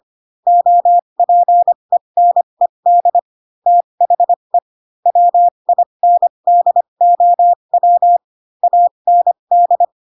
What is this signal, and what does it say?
Morse code, heard as a keyed tone, about 25 wpm, spaced as for 22 wpm, 700 Hz. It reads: OPENED THE WINDOW AND
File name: Pinocchio / Pinocchio ch6 18 25 ik1hge